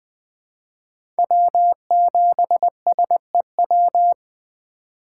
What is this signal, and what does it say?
Morse code, heard as a keyed tone, about 20 wpm, 700 Hz. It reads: W7SEW
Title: USA Call Signs - 5 Characters - 3x Character Spacing - 20wpm